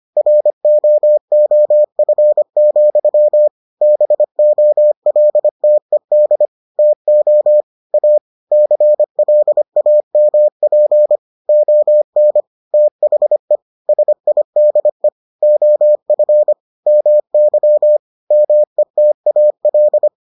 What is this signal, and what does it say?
Morse code, heard as a keyed tone, about 25 wpm, 600 Hz. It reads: ROOF, BOLTED TO A CLAMP ON THE SIDE OF MY METAL